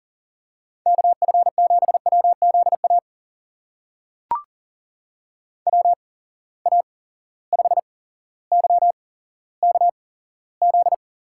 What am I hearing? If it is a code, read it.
Morse code, heard as a keyed tone, about 40 wpm, spaced as for 17 wpm, 700 Hz. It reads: KF7WZA WA5YKZ
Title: USA Call Signs - 6 Characters - 8x Character Spacing - 40wpm